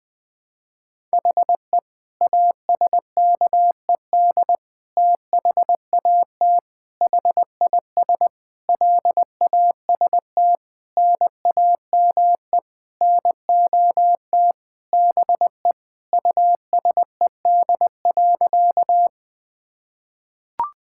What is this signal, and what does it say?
Morse code, heard as a keyed tone, about 21 wpm, 700 Hz. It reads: HE ASKED THAT HIS LAST NAME NOT BE USED.
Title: Sentences with 9-Word Phrases - 20wpm